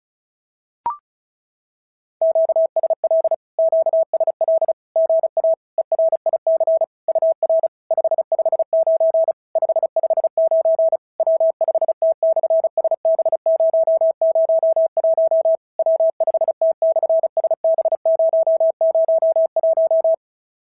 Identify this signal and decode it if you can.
Morse code, heard as a keyed tone, about 36 wpm, 650 Hz. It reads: QSL QSL GA ERIC UR 559 559 W5T/SB001 W5T/SB001
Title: SOTA QSO Practice 35wpm